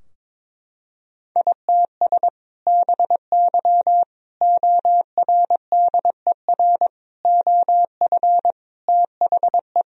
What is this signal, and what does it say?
Morse code, heard as a keyed tone, about 22 wpm, 700 Hz. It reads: ITS BY ORDER OF THE
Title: Sentences from Top 1000 Words - Review Part 3 - 22wpm